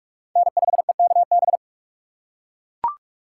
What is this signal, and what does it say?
Morse code, heard as a keyed tone, about 39 wpm, 700 Hz. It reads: N5EKB